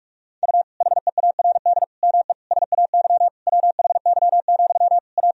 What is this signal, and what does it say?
Morse code, heard as a keyed tone, about 46 wpm, 690 Hz. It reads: U HEARD ME SAY WHY, A
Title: Sentences from Top 500 Words and Abbreviations 45wpm